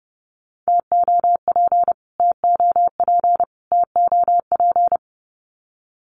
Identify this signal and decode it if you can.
Morse code, heard as a keyed tone, about 30 wpm, 700 Hz. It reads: TOP TOP TOP
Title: Top 500 Words Repeated 30wpm